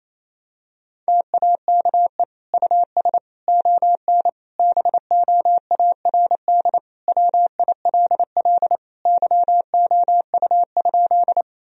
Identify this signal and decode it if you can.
Morse code, heard as a keyed tone, about 28 wpm, 700 Hz. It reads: TAKE US ON BOARD WILL YOU?
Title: Sentences from Top 1000 Words - Review Part 4 - 28wpm